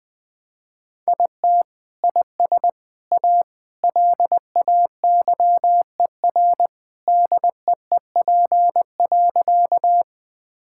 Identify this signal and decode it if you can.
Morse code, heard as a keyed tone, about 20 wpm, 700 Hz. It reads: IT IS A LAYER DEEP.
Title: Sentences from Top 1600 Words 20wpm